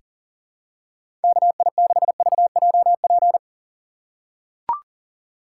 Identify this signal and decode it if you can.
Morse code, heard as a keyed tone, about 40 wpm, 700 Hz. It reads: KI6VJP
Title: USA Call Signs - 6 Characters - 4x Character Spacing - 40wpm